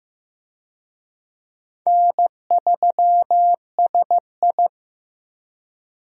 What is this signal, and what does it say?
Morse code, heard as a keyed tone, about 15 wpm, 700 Hz. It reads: N3SI
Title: USA Call Signs - 4 Characters - Repeated 2x - 15wpm